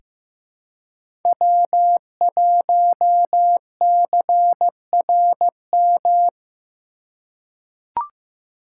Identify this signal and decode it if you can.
Morse code, heard as a keyed tone, about 16 wpm, 700 Hz. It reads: W1CRM
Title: USA Call Signs - 3x Character Spacing - 15wpm